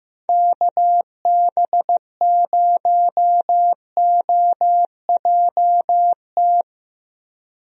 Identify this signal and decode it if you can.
Morse code, heard as a keyed tone, about 15 wpm, 700 Hz. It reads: KB0OJT
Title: USA Call Signs - 6 Characters - Repeated 2x - 15wpm